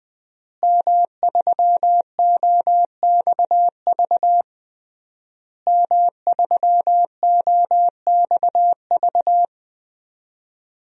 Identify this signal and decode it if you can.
Morse code, heard as a keyed tone, about 20 wpm, 700 Hz. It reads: M3OXV M3OXV
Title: DXCC-100 Call Signs - 5 Characters - Repeated 3x - 20wpm